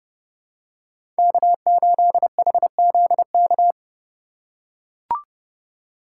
Morse code, heard as a keyed tone, about 31 wpm, 700 Hz. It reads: K8HZK
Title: USA Call Signs - 5 Characters - 2x Character Spacing - 30wpm